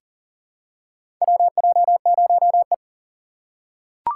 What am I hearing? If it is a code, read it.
Morse code, heard as a keyed tone, about 40 wpm, 700 Hz. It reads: WJ0E
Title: USA Call Signs - 4 Characters - 6x Character Spacing - 40wpm